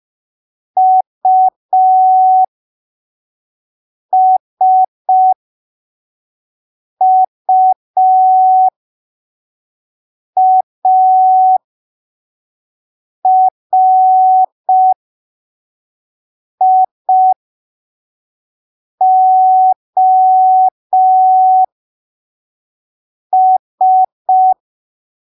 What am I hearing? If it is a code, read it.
Morse code, heard as a keyed tone, about 5 wpm, spaced as for 3 wpm, 750 Hz. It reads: USUARIOS
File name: arquivos / texto / texto 06 05ppm